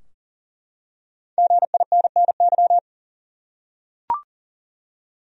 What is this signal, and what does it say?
Morse code, heard as a keyed tone, about 40 wpm, 700 Hz. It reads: GINNY
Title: USA Common Names - 40wpm